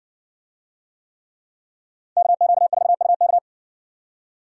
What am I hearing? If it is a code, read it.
Morse code, heard as a keyed tone, about 58 wpm, 700 Hz. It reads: KQ4UK